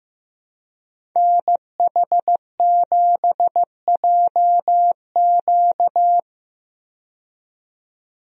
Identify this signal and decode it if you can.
Morse code, heard as a keyed tone, about 15 wpm, 700 Hz. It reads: NH7JQ
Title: USA Call Signs - 5 Characters - 15wpm